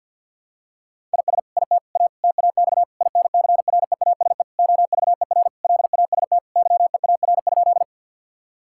Morse code, heard as a keyed tone, about 52 wpm, 700 Hz. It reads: IS IT A TAX INCREASE OVER LAST YEAR?
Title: Sentences from Top 1000 Words - Review Part 5 - 50wpm